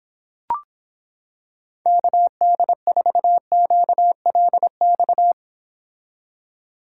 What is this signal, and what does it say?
Morse code, heard as a keyed tone, about 26 wpm, 700 Hz. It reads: KD4QLX